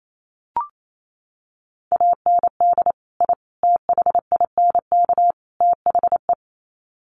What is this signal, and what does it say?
Morse code, heard as a keyed tone, about 28 wpm, 700 Hz. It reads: AND I THINK THE
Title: Sets of 4 Words 28wpm